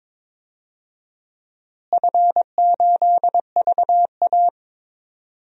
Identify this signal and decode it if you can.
Morse code, heard as a keyed tone, about 22 wpm, 700 Hz. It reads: F8VA
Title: DXCC-100 Call Signs - 4 Characters - Repeated 2x - 22wpm